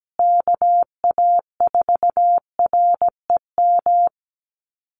Morse code, heard as a keyed tone, about 17 wpm, 700 Hz. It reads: KA4REM
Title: USA Call Signs Repeated 2x 17wpm